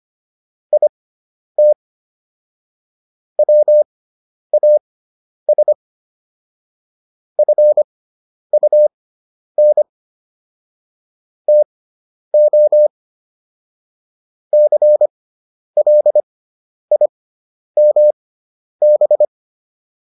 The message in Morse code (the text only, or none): IT WAS FUN TO CLIMB